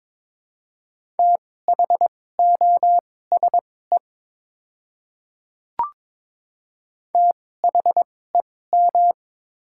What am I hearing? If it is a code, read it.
Morse code, heard as a keyed tone, about 22 wpm, spaced as for 16 wpm, 700 Hz. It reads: THOSE THEM
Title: 2x Character Spacing - Top 300 Words - 22wpm